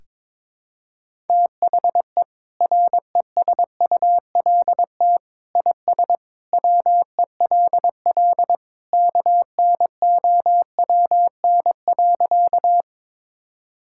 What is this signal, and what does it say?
Morse code, heard as a keyed tone, about 22 wpm, 700 Hz. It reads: THE RESULT IS WELL KNOWN.
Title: Sentences from Top 1000 Words - Review Part 5 - 22wpm